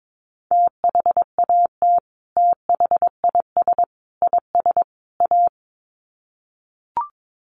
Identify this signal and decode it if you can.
Morse code, heard as a keyed tone, about 22 wpm, 700 Hz. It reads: THAT THIS IS A
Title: Sets of 4 Words 22wpm